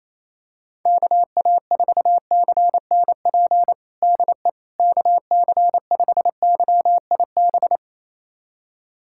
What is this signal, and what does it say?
Morse code, heard as a keyed tone, about 28 wpm, 700 Hz. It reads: KA4CNP DE KC5YIB
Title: USA Callsign DE Callsign - 6 Characters - 28wpm